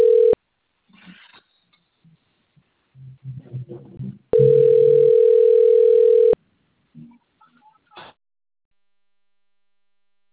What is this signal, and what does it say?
Ringback tone (North America), 440 + 480 Hz, with speech.
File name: rg-1600-unknown- - - .506